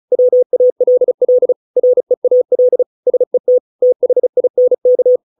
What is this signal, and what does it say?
Morse code, heard as a keyed tone, about 34 wpm, 500 Hz. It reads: WALL REAL SET THINK